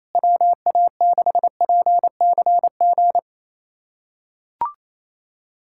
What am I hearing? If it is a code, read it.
Morse code, heard as a keyed tone, about 28 wpm, 700 Hz. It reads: WA6PCG
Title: USA Callsign DE Callsign 28wpm